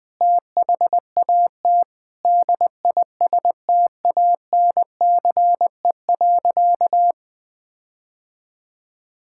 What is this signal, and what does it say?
Morse code, heard as a keyed tone, about 20 wpm, 700 Hz. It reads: THAT DISTANCE.
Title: Sentences from Top 1600 Words 20wpm